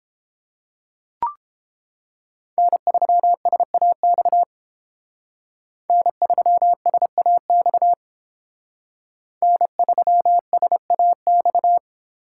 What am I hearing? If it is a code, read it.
Morse code, heard as a keyed tone, about 31 wpm, 700 Hz. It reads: N3SAX N3SAX N3SAX